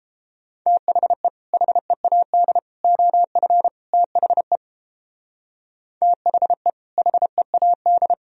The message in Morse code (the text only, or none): THE HEAD OF THE THE HEAD